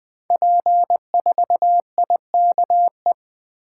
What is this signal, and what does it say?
Morse code, heard as a keyed tone, about 20 wpm, 700 Hz. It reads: P4IKE
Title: Code Groups - 5 Letter:Numbers - 20wpm